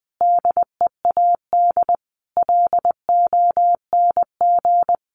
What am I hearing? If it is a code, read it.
Morse code, heard as a keyed tone, about 20 wpm, 700 Hz. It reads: DEAD LONG
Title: Sentences from Top 1500 Words 20wpm